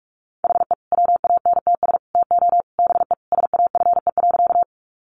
Morse code, heard as a keyed tone, about 44 wpm, 700 Hz. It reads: HE WANTS TO BE SAFE.
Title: Additional Easy Phrases 45wpm